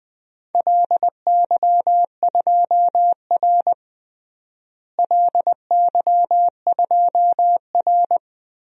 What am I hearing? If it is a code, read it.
Morse code, heard as a keyed tone, about 20 wpm, 700 Hz. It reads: LY2R LY2R
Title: DXCC-100 Call Signs - 4 Characters - Repeated 2x - 20wpm